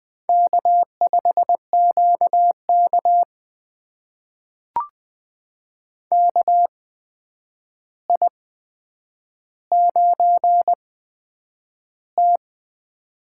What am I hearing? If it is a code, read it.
Morse code, heard as a keyed tone, about 20 wpm, spaced as for 9 wpm, 700 Hz. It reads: K5QK KI9T
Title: USA Call Signs - 4 Characters - 8x Character Spacing - 20wpm